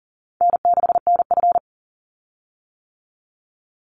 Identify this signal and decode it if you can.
Morse code, heard as a keyed tone, about 40 wpm, 700 Hz. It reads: N6NF
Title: Easy Call Signs - 40wpm